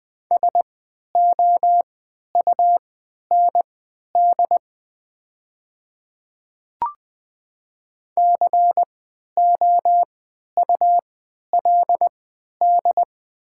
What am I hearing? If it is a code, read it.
Morse code, heard as a keyed tone, about 20 wpm, spaced as for 11 wpm, 700 Hz. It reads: SOUND COULD
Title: 3x Character Spacing - Top 100 Words - 20wpm